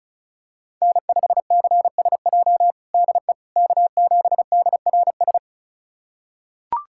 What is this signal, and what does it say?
Morse code, heard as a keyed tone, about 34 wpm, 700 Hz. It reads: N5CSJ DE K7DRS
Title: USA Callsign DE Callsign - 5 Characters - 35wpm